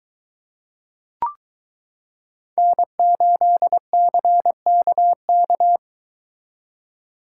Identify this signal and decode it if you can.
Morse code, heard as a keyed tone, about 23 wpm, 700 Hz. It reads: N8CKK